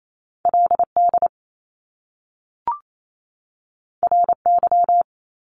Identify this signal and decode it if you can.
Morse code, heard as a keyed tone, about 28 wpm, 700 Hz. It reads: LD RY
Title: Top Two Letter Combinations in Words 28wpm